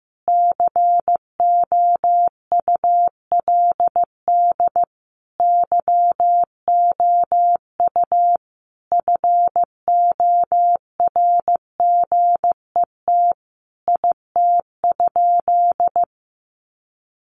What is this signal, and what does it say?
Morse code, heard as a keyed tone, about 15 wpm, 700 Hz. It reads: COULD YOU FORGET IT?
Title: Sentences from Top 1200 Words 15wpm